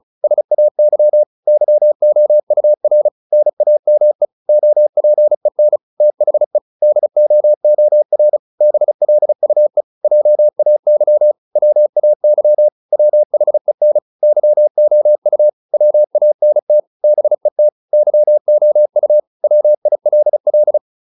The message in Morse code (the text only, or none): SAY YOUR NAME OPEN THE DOOR BLUE JAY WAY WHEN YOU WANT BET YOU WILL